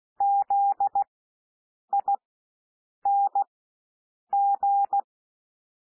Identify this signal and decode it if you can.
Morse code, heard as a keyed tone, about 16 wpm, spaced as for 7 wpm, 800 Hz. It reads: ZING
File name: cwmp3 / W1 SOUND WORDS 16 wpm